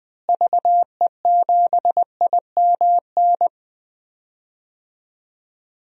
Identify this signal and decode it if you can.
Morse code, heard as a keyed tone, about 20 wpm, 700 Hz. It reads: VE7IMN
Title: DXCC-100 Call Signs - 6 Characters - Repeated 2x - 20wpm